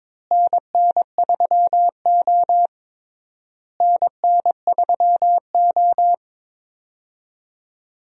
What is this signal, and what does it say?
Morse code, heard as a keyed tone, about 22 wpm, 700 Hz. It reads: NN3O NN3O
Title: USA Call Signs - 4 Characters - Repeated 3x - 22wpm